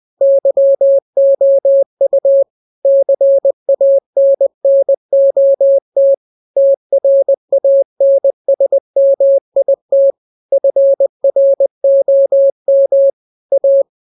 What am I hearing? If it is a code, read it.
Morse code, heard as a keyed tone, about 20 wpm, 550 Hz. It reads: YOU CANNOT TRANSMIT FROM A
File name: jokes / joke 20 20 0000